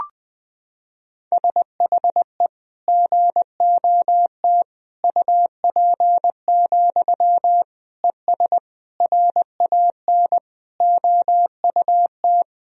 Morse code, heard as a keyed tone, about 21 wpm, 700 Hz. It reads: SHE GOT UP, ES RAN OUT
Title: Sentences from Top 500 Words and Abbreviations 20wpm